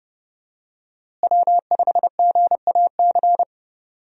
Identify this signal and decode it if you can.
Morse code, heard as a keyed tone, about 30 wpm, 700 Hz. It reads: W5GAC
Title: USA Call Signs - 5 Characters - Repeated 2x - 30wpm